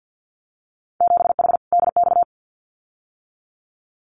Morse code, heard as a keyed tone, about 50 wpm, 700 Hz. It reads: 7H DX
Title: Winter Field Day Exchanges Encore 50wpm